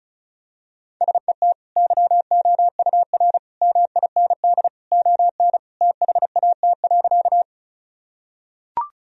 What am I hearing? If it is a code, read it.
Morse code, heard as a keyed tone, about 36 wpm, 700 Hz. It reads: SET YOUR MIND ON THAT.
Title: Sentences from Top 1000 Words - Review Part 1 - 35wpm